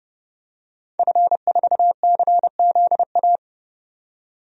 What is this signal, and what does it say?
Morse code, heard as a keyed tone, about 30 wpm, 700 Hz. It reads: F4CZA
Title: DXCC-100 Call Signs - 5 Characters - 30wpm